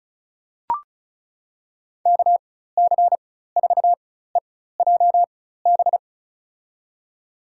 Morse code, heard as a keyed tone, about 36 wpm, spaced as for 16 wpm, 700 Hz. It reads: KC4EJB
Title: USA Call Signs - 6 Characters - 4x Character Spacing - 35wpm